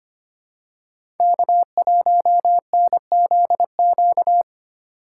Morse code, heard as a keyed tone, about 25 wpm, 700 Hz. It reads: K1NZQ
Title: USA Call Signs - 5 Characters - 2x Character Spacing - 25wpm